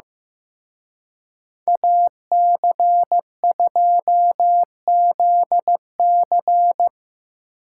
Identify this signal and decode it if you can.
Morse code, heard as a keyed tone, about 18 wpm, 700 Hz. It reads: AC2ZC
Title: Speed Racing - USA Call Signs 15wpm